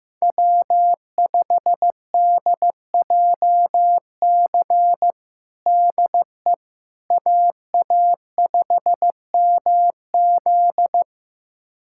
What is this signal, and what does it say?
Morse code, heard as a keyed tone, about 15 wpm, 700 Hz. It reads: W5DJC DE AA5MZ
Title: USA Callsign DE Callsign - 5 Characters - 15wpm